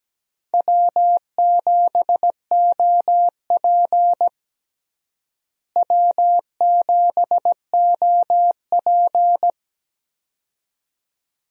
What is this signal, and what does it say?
Morse code, heard as a keyed tone, about 17 wpm, 700 Hz. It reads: W7OP W7OP
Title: USA Call Signs - 4 Characters - Repeated 2x - 17wpm